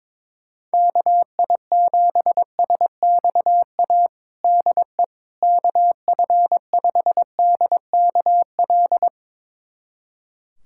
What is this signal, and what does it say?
Morse code, heard as a keyed tone, about 22 wpm, 700 Hz. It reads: KI7SXA DE KF5DKL
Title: USA Callsign DE Callsign - 6 Characters - 22wpm